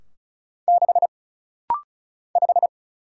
Morse code, heard as a keyed tone, about 36 wpm, 700 Hz. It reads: 6 5
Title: Single Letter-Number - Rapid-Fire 35wpm